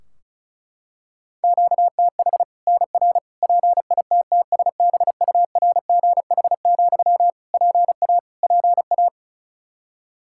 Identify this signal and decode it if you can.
Morse code, heard as a keyed tone, about 34 wpm, 700 Hz. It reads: QTH NR PITTSBURGH, PA PA
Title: Practice QSO - Exchange 1 - Beginning - 35wpm